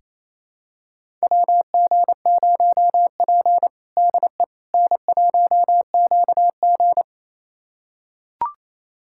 Morse code, heard as a keyed tone, about 28 wpm, 700 Hz. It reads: WG0P DE N1QG
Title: USA Callsign DE Callsign - 4 Characters - 28wpm